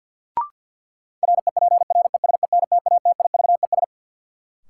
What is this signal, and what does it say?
Morse code, heard as a keyed tone, about 50 wpm, 700 Hz. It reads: REPRESENTATIVES
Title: Sets of 5 Words Encore - New Word Drill - 50wpm